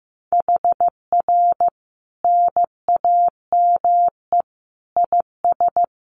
Morse code, heard as a keyed tone, about 15 wpm, 700 Hz. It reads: HR NAME IS
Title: Common QSO Phrases 15wpm